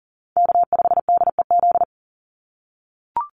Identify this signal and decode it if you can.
Morse code, heard as a keyed tone, about 40 wpm, 700 Hz. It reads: K5DEZ
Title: NA SOTA Chaser Callsigns 40wpm